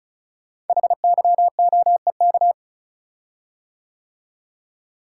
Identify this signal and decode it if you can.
Morse code, heard as a keyed tone, about 36 wpm, 700 Hz. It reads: HYOEK
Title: Code Groups - 5 Letters - 35wpm